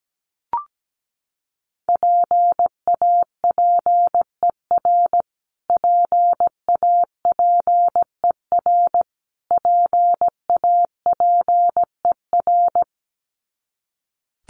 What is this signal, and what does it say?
Morse code, heard as a keyed tone, about 17 wpm, 700 Hz. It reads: PAPER PAPER PAPER
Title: Top 300 Words Repeated 17wpm